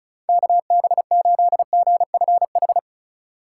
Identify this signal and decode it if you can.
Morse code, heard as a keyed tone, about 42 wpm, 700 Hz. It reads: KB8GFH
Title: Speed Racing - USA 2x3 Call Signs 35wpm